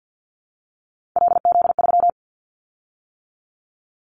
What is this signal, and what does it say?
Morse code, heard as a keyed tone, about 50 wpm, 700 Hz. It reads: LZ3
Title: Callsign Prefixes for Top 50 DXCC - 50wpm